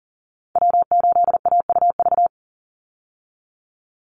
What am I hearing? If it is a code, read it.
Morse code, heard as a keyed tone, about 40 wpm, 700 Hz. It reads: W8AUV
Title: USA Call Signs 40wpm